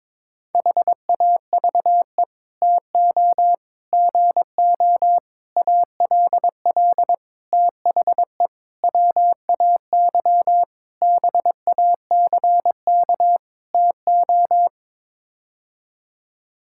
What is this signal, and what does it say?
Morse code, heard as a keyed tone, about 22 wpm, 700 Hz. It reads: HAVE TO GO ALL THE WAY BACK TO
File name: practice / 22wpm / Sets of 8 Words - 22wpm